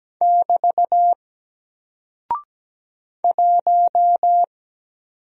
Morse code, heard as a keyed tone, about 17 wpm, 700 Hz. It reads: <BT> 1
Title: Single Character (Letter:Number:Punctuation:Prosign) 17wpm